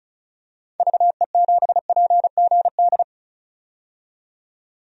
Morse code, heard as a keyed tone, about 34 wpm, 700 Hz. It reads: VE7PGD
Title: DXCC-100 Call Signs - 6 Characters - 35wpm